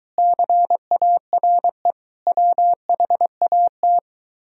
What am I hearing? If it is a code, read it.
Morse code, heard as a keyed tone, about 22 wpm, 700 Hz. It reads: CARE WHAT